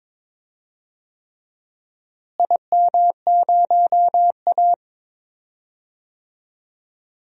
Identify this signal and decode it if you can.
Morse code, heard as a keyed tone, about 22 wpm, 700 Hz. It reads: IM0A
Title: DXCC-100 Call Signs - 4 Characters - Repeated 3x - 22wpm